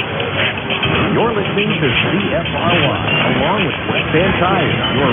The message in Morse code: none

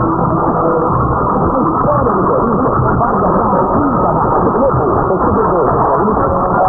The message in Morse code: none

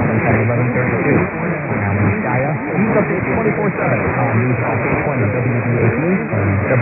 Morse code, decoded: none

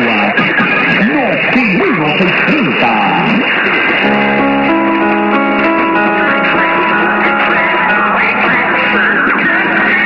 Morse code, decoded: none

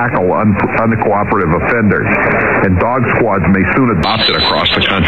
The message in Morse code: none

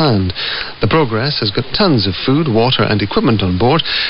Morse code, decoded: none

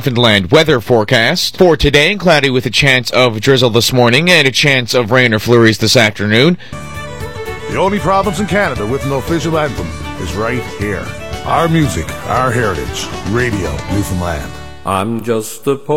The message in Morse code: none

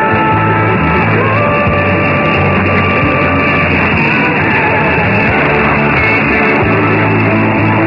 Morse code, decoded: none

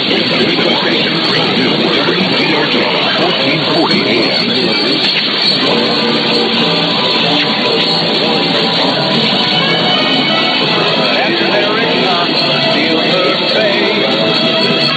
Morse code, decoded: none

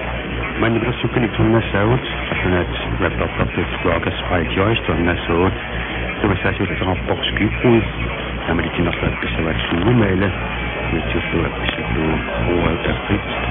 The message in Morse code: none

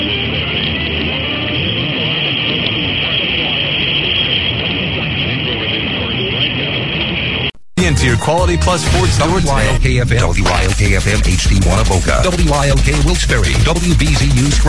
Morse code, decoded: none

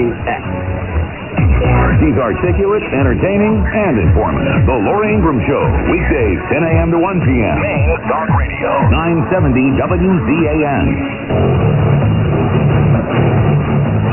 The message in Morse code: none

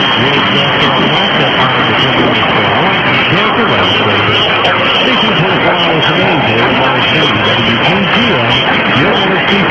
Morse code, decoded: none

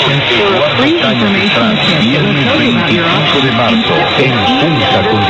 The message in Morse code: none

